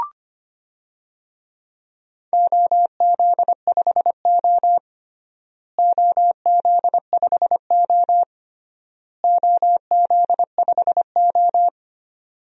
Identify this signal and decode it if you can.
Morse code, heard as a keyed tone, about 25 wpm, 700 Hz. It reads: OZ5O OZ5O OZ5O
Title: DXCC-100 Call Signs - 4 Characters - Repeated 3x - 25wpm